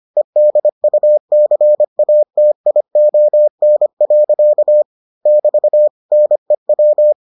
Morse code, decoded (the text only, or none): EDUCATION. <BT> NEW